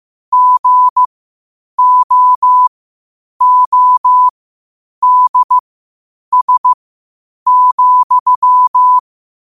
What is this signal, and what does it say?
Morse code, heard as a keyed tone, about 15 wpm, spaced as for 8 wpm, 1000 Hz. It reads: GOODS,
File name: qotdfiles / 8wpm